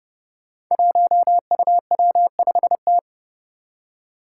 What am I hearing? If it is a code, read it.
Morse code, heard as a keyed tone, about 30 wpm, 700 Hz. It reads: 1UW5T
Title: Five Challenging Letter-Number Combinations 30wpm